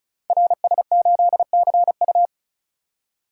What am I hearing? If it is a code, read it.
Morse code, heard as a keyed tone, about 36 wpm, 700 Hz. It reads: RS8CU
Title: Code Groups - 5 Letter:Numbers - 35wpm